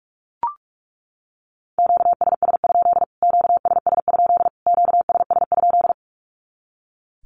Morse code, heard as a keyed tone, about 44 wpm, 700 Hz. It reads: QSS? QSS? QSS?
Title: Q-Codes Repeated 3x 45wpm